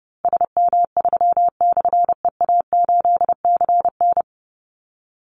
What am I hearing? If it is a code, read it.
Morse code, heard as a keyed tone, about 30 wpm, 700 Hz. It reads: SM3/EA8CN
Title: Call Signs Repeated 2x - 30wpm